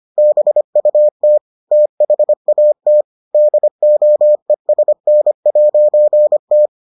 Morse code, heard as a keyed tone, about 25 wpm, 600 Hz. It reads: BUT THAT DOESN'T